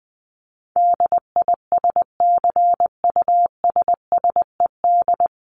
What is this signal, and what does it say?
Morse code, heard as a keyed tone, about 19 wpm, 700 Hz. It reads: DISCUSSED
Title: ED Suffix - Top 100 Words with Base Words 20wpm